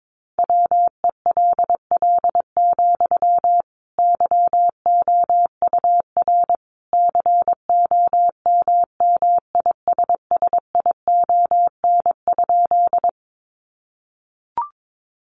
Morse code, heard as a keyed tone, about 22 wpm, 700 Hz. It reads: WELL, YOUR COMMISSION?
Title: Sentences from Top 1400 Words 22wpm